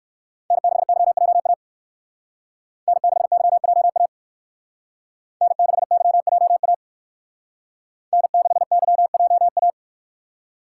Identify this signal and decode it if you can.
Morse code, heard as a keyed tone, about 57 wpm, 700 Hz. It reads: N6YJA N6YJA N6YJA N6YJA